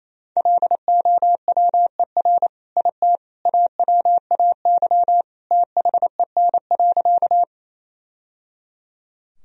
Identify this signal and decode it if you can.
Morse code, heard as a keyed tone, about 27 wpm, 700 Hz. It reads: LOWER IT AWAY THEN.
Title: Sentences from Top 1000 Words - Review Part 4 - 28wpm